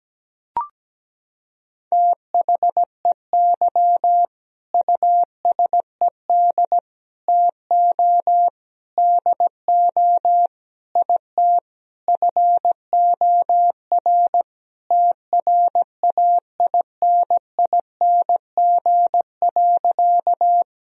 Morse code, heard as a keyed tone, about 17 wpm, 700 Hz. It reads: THEY USED TO DO IT FOR TRAINING.
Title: Sentences from Top 1000 Words - Review Part 4 - 17wpm